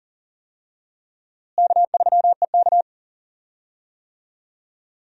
Morse code, heard as a keyed tone, about 39 wpm, 700 Hz. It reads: K3EK
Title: USA Call Signs - 4 Characters - Repeated 3x - 40wpm